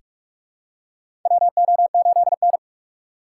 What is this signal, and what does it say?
Morse code, heard as a keyed tone, about 44 wpm, 700 Hz. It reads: WO8N
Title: USA Call Signs - 6x Character Spacing - 45wpm